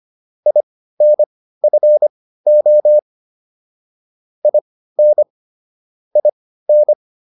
Morse code, heard as a keyed tone, about 25 wpm, spaced as for 15 wpm, 600 Hz. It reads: INFO IN IN